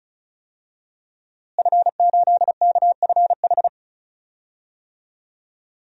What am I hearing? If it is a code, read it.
Morse code, heard as a keyed tone, about 34 wpm, 700 Hz. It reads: F8KFH
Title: DXCC-100 Call Signs - 5 Characters - Repeated 2x - 35wpm